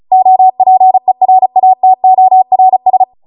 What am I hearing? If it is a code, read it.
Morse code, heard as a keyed tone, about 36 wpm, 750 Hz. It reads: OPERATORS